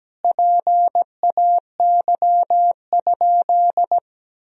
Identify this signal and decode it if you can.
Morse code, heard as a keyed tone, about 17 wpm, 700 Hz. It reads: PAY?
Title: Sentences from Top 1000 Words - Review Part 4 - 17wpm